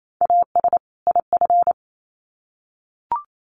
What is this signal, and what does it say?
Morse code, heard as a keyed tone, about 28 wpm, 700 Hz. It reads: AS IF
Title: Sets of 2 Words 28wpm